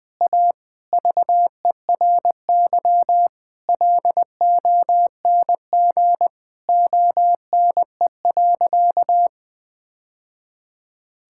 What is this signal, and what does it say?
Morse code, heard as a keyed tone, about 21 wpm, 700 Hz. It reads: A VERY LONG ONE.
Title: Sentences from Top 1000 Words - Review Part 5 - 20wpm